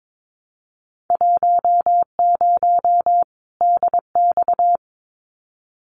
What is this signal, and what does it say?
Morse code, heard as a keyed tone, about 22 wpm, 700 Hz. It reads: 10 DX
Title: Winter Field Day Exchanges 22wpm